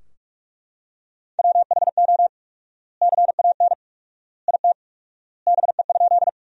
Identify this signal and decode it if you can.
Morse code, heard as a keyed tone, about 42 wpm, 700 Hz. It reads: WHO CAN IT BE?
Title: 4x Word-Spacing - Sentences from Top 100 Words - 45wpm